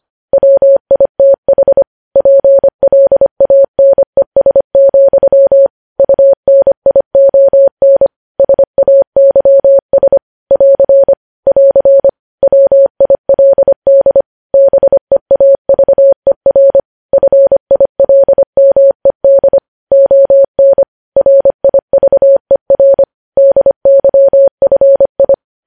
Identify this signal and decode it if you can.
Morse code, heard as a keyed tone, about 25 wpm, 550 Hz. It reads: WITH PLANES, UNION SAYS <AR> <AR> WILD BEAVER FILMED ON RIVER DYFI
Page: News Headlines in Morse Code at 25 WPM